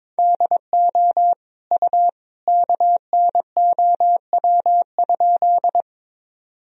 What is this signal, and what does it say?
Morse code, heard as a keyed tone, about 22 wpm, 700 Hz. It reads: DO U KNOW?
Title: Sentences from Top 100 Words and Abbreviations 22wpm